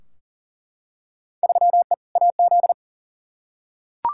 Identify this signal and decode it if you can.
Morse code, heard as a keyed tone, about 37 wpm, 700 Hz. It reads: 3E AZ
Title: Field Day Exchanges - 40wpm